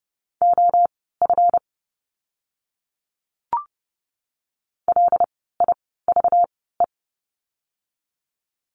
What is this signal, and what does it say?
Morse code, heard as a keyed tone, about 30 wpm, spaced as for 17 wpm, 700 Hz. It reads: OF LIVE
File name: course / 30:10wpm / 237 Lesson 38f - Letter q - Words 30wpm